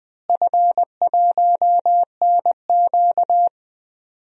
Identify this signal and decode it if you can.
Morse code, heard as a keyed tone, about 20 wpm, 700 Hz. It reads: F1NQ
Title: DXCC-100 Call Signs - 4 Characters - Repeated 2x - 20wpm